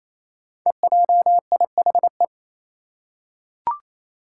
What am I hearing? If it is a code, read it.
Morse code, heard as a keyed tone, about 28 wpm, 700 Hz. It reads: EJIHE
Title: Five Challenging Letter-Number Combinations 28wpm